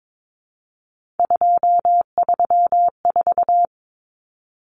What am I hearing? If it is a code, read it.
Morse code, heard as a keyed tone, about 22 wpm, 700 Hz. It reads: 234
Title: Numbers - 3-Digit 22wpm